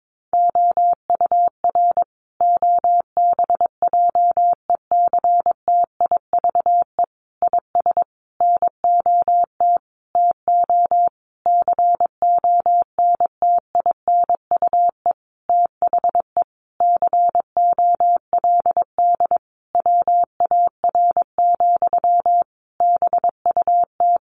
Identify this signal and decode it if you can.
Morse code, heard as a keyed tone, about 22 wpm, 700 Hz. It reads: OUR OBJECTIVE IS NOT TO CONTINUE THE COLD WAR, BUT
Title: Sentences from Top 1200 Words 22wpm